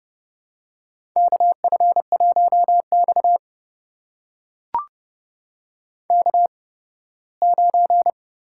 Morse code, heard as a keyed tone, about 28 wpm, spaced as for 18 wpm, 700 Hz. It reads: KF1X K9
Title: USA Call Signs - 4 Characters - 8x Character Spacing - 30wpm